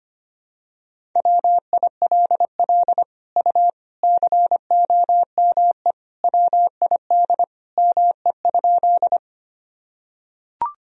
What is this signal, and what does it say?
Morse code, heard as a keyed tone, about 25 wpm, 700 Hz. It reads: WILL U COME WID ME?
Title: Sentences from Top 200 Words and Abbreviations 25wpm